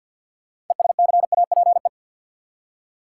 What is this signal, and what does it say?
Morse code, heard as a keyed tone, about 52 wpm, 700 Hz. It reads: ESCAPE
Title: Six Letter Words - 50wpm